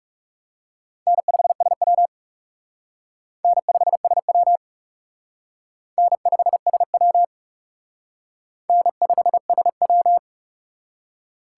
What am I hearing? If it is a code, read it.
Morse code, heard as a keyed tone, about 36 wpm, 700 Hz. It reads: N5SW N5SW N5SW N5SW
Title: Speed Racing - USA 1x2 Call Signs 30wpm